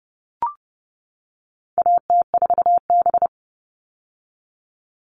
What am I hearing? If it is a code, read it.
Morse code, heard as a keyed tone, about 30 wpm, 700 Hz. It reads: AT4B
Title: All Cut Numbers - 4-Digit 30wpm